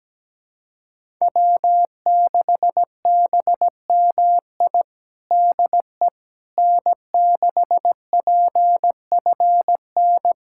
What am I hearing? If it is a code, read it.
Morse code, heard as a keyed tone, about 17 wpm, 700 Hz. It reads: W6BMI DE N6PFN
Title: USA Callsign DE Callsign - 5 Characters - 17wpm